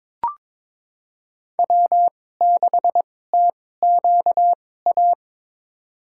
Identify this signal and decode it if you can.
Morse code, heard as a keyed tone, about 22 wpm, spaced as for 16 wpm, 700 Hz. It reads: W6TQA
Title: USA Call Signs - 5 Characters - 2x Character Spacing - 22wpm